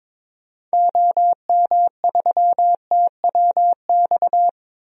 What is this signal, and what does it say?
Morse code, heard as a keyed tone, about 22 wpm, 700 Hz. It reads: OM3TWX
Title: DXCC-100 Call Signs - 6 Characters - 22wpm